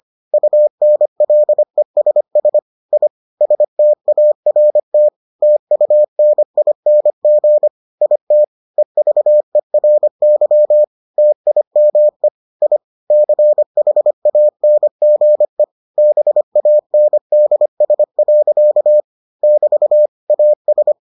Morse code, heard as a keyed tone, about 25 wpm, 600 Hz. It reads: UNLESS I START TUNING IT EVERY TIME I CHANGE BANDS. <BT> AS